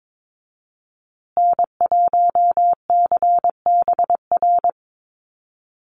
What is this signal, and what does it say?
Morse code, heard as a keyed tone, about 22 wpm, 700 Hz. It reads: N1CBR
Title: USA Call Signs Repeated 3x 22wpm